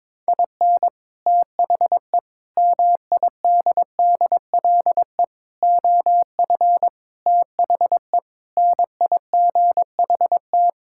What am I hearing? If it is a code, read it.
Morse code, heard as a keyed tone, about 22 wpm, 700 Hz. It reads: IN THE MIDDLE OF THE NIGHT
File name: practice / 22wpm / Sets of 6 Words - 22wpm